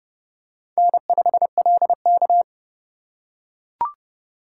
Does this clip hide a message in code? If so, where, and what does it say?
Yes: Morse code from 0.8 to 2.4 s, heard as a keyed tone, about 27 wpm, 700 Hz. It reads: N5LK